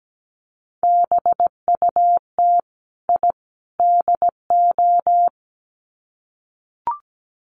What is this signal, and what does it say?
Morse code, heard as a keyed tone, about 17 wpm, 700 Hz. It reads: BUT I DO
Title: Sets of 3 Words 17wpm